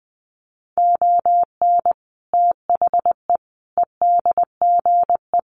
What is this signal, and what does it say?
Morse code, heard as a keyed tone, about 19 wpm, 700 Hz. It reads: ON THE EDGE
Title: Sets of 5 Words Encore2 20wpm